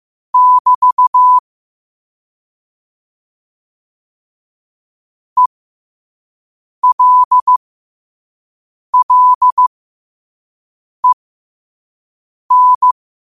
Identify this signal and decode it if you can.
Morse code, heard as a keyed tone, about 15 wpm, spaced as for 5 wpm, 1000 Hz. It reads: <BT> ELLEN
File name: qotdfiles / 5wpm